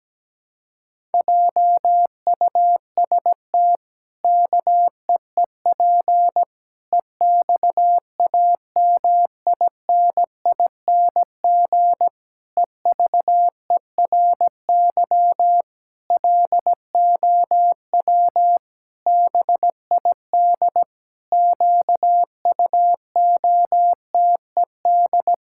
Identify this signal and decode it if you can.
Morse code, heard as a keyed tone, about 17 wpm, 700 Hz. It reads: JUST KEEP EXAMINING EVERY LOW BID QUOTED
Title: Pangram Challenge v2 - 17wpm